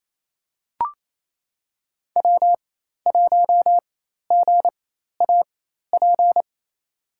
Morse code, heard as a keyed tone, about 28 wpm, spaced as for 13 wpm, 700 Hz. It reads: W1GAP
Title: USA Call Signs - 5 Characters - 4x Character Spacing - 28wpm